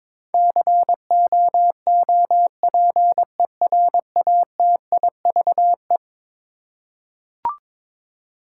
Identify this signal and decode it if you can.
Morse code, heard as a keyed tone, about 22 wpm, 700 Hz. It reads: COOPERATIVE
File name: practice / 22wpm / Eleven Letter Words - 22wpm